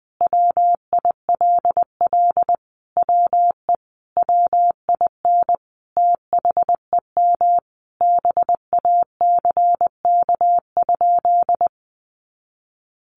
Morse code, heard as a keyed tone, about 19 wpm, 700 Hz. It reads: WILL WE WIN THEM BACK?
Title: Sentences from Top 1500 Words 20wpm